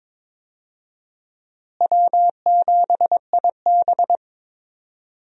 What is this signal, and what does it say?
Morse code, heard as a keyed tone, about 22 wpm, 700 Hz. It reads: W7IB
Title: USA Call Signs - 4 Characters - Repeated 2x - 22wpm